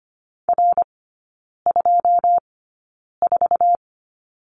Morse code, heard as a keyed tone, about 25 wpm, spaced as for 9 wpm, 690 Hz. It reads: R24
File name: Lesson 14e - Number 2 - Three Characters 25wpm